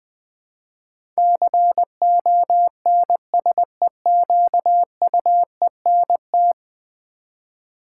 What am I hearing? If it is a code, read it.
Morse code, heard as a keyed tone, about 19 wpm, 700 Hz. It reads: CONSEQUENT
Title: Build A Word - 5 Letters At A Time - 12-Letter Words - 20wpm